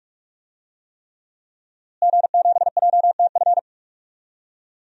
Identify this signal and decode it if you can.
Morse code, heard as a keyed tone, about 42 wpm, 700 Hz. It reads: G7JTF
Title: DXCC-100 Call Signs - 5 Characters - Repeated 3x - 45wpm